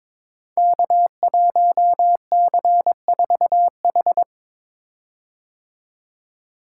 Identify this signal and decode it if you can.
Morse code, heard as a keyed tone, about 22 wpm, 700 Hz. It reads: K1C4H
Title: Code Groups - 5 Letter:Numbers - 22wpm